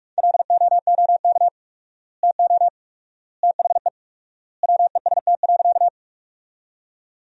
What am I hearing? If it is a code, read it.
Morse code, heard as a keyed tone, about 46 wpm, 700 Hz. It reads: LOOK TO THE WEST.
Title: 4x Word-Spacing - Sentences from Top 400 Words - 45wpm